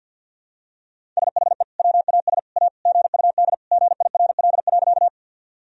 Morse code, heard as a keyed tone, about 52 wpm, 700 Hz. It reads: SHE WAS A GUD GIRL.
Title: Sentences from Top 400 Words and Abbreviations 50wpm